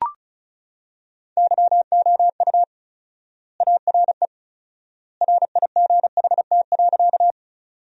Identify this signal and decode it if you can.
Morse code, heard as a keyed tone, about 36 wpm, 700 Hz. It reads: YOU ARE RIGHT.
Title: 4x Word-Spacing - Sentences from Top 200 Words - 35wpm